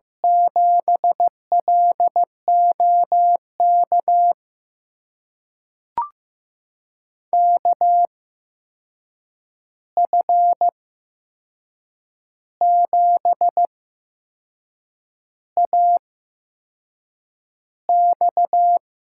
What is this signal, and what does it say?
Morse code, heard as a keyed tone, about 15 wpm, spaced as for 6 wpm, 700 Hz. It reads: 7LOK KF7AX
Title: USA Call Signs - 6 Characters - 8x Character Spacing - 15wpm